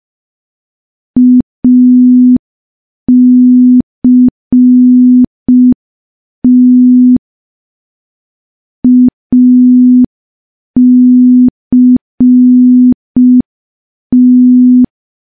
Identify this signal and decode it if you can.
Morse code, heard as a keyed tone, about 5 wpm, 250 Hz. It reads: ACT ACT
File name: morse / morse-training-05WPM